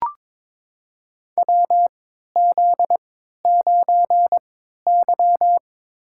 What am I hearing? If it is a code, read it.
Morse code, heard as a keyed tone, about 23 wpm, 700 Hz. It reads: W Z 9 Y
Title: USA Call Signs - 4 Characters - 3x Character Spacing - 22wpm